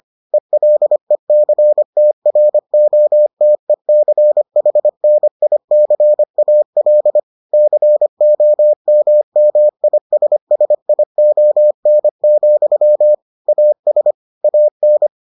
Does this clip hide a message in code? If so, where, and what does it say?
Yes: Morse code from 0.3 to 15.1 s, heard as a keyed tone, about 25 wpm, 600 Hz. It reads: ELECTROTECHNICAL COMMISSION, AS AN